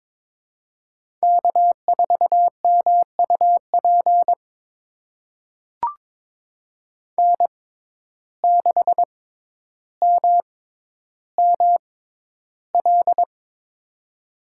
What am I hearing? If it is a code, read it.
Morse code, heard as a keyed tone, about 22 wpm, spaced as for 11 wpm, 700 Hz. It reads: K4MUP N6MML
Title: USA Call Signs - 5 Characters - 6x Character Spacing - 22wpm